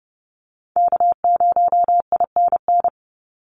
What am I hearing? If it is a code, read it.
Morse code, heard as a keyed tone, about 30 wpm, 700 Hz. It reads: K0INN
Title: Call Signs - 30wpm